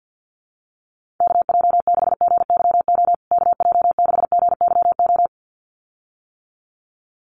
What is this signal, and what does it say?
Morse code, heard as a keyed tone, about 48 wpm, 700 Hz. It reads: KJ6GYO KJ6GYO
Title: USA Call Signs Repeated 2x 50wpm